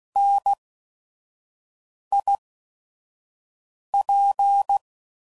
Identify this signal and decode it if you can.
Morse code, heard as a keyed tone, about 16 wpm, spaced as for 5 wpm, 800 Hz. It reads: NIP
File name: Prefixes / By First Letter / Ns